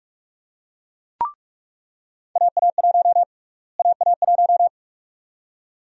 Morse code, heard as a keyed tone, about 42 wpm, 700 Hz. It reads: AA1 AA1
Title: USA Call Sign Prefixes - Repeated 2x - 45wpm